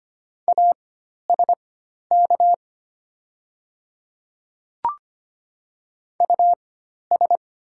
Morse code, heard as a keyed tone, about 25 wpm, spaced as for 12 wpm, 690 Hz. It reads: ASK US
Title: 4x Character Spacing - Top 200 Words - 25wpm